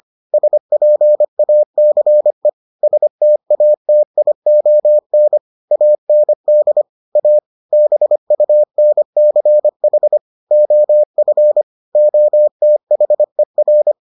Morse code, heard as a keyed tone, about 25 wpm, 600 Hz. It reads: SPACE STATION AND A BUNCH OF OTHER